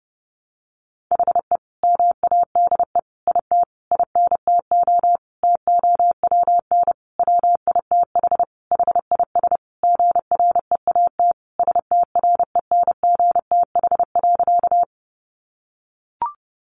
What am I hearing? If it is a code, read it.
Morse code, heard as a keyed tone, about 30 wpm, 700 Hz. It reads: HE MADE IT INTO TOWN WITH HIS GREAT STRENGTH.
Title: Sentences from Top 1300 Words 30wpm